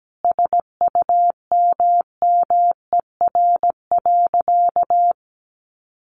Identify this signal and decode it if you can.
Morse code, heard as a keyed tone, about 17 wpm, 700 Hz. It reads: SUMMER.
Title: Sentences from Top 1100 Words 17wpm